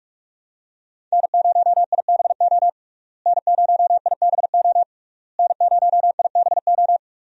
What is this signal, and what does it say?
Morse code, heard as a keyed tone, about 46 wpm, 700 Hz. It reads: N0IBO N0IBO N0IBO
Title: USA Call Signs - 5 Characters - Repeated 3x - 45wpm